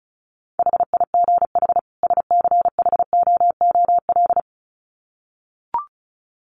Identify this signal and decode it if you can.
Morse code, heard as a keyed tone, about 34 wpm, 700 Hz. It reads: HIGH SCHOOL
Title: Sets of 2 Words 35wpm